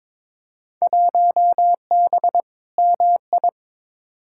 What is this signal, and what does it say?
Morse code, heard as a keyed tone, about 22 wpm, 700 Hz. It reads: 1B MI
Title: Field Day Exchanges - 22wpm